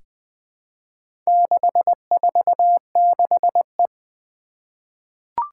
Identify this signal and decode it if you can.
Morse code, heard as a keyed tone, about 20 wpm, 700 Hz. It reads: 646E
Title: Four Challenging Letter-Number Combinations 20wpm